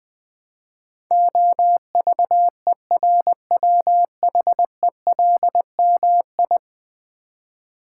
Morse code, heard as a keyed tone, about 20 wpm, 700 Hz. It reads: OVERWHELMI
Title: Build A Word - 5 Letters At A Time - 12-Letter Words - 20wpm